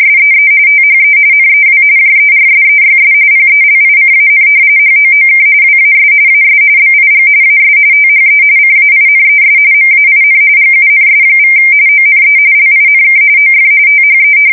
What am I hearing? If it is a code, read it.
RTTY, about 45.45 baud. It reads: SEE THE NEW JERSEYS PELICAN HOOK WAS AS BIG AS THE CAINES MAIN GUNS= ONE LINK OF THE
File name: fsk170 / battleshipnj